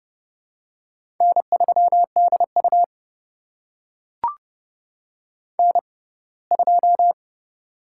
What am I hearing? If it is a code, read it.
Morse code, heard as a keyed tone, about 30 wpm, spaced as for 20 wpm, 700 Hz. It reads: N3DU N2
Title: USA Call Signs - 6x Character Spacing - 30wpm